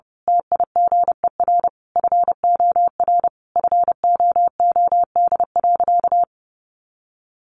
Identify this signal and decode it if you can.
Morse code, heard as a keyed tone, about 30 wpm, 700 Hz. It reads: TIGER FOR FOOD.
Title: A Sound Of Thunder 30wpm